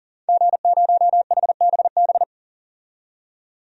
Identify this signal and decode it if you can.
Morse code, heard as a keyed tone, about 40 wpm, 700 Hz. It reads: G0HBB
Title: DXCC-100 Call Signs - 5 Characters - 40wpm